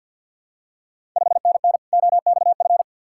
Morse code, heard as a keyed tone, about 52 wpm, 700 Hz. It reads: 5NN OXF
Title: Ontario QSO Party Exchange 50wpm